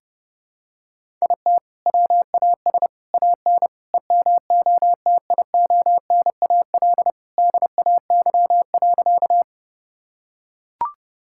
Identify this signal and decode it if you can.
Morse code, heard as a keyed tone, about 31 wpm, 700 Hz. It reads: IT WAS AN EMOTIONAL DAY.
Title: Sentences from Top 1000 Words - Review Part 5 - 30wpm